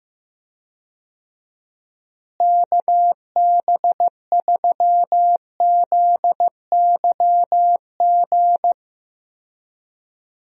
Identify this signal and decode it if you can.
Morse code, heard as a keyed tone, about 15 wpm, 700 Hz. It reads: KB3ZYG
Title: USA Call Signs - 6 Characters - Repeated 3x - 15wpm